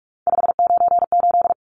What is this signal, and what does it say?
Morse code, heard as a keyed tone, about 44 wpm, 700 Hz. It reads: 598
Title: Numbers - 3-Digit 45wpm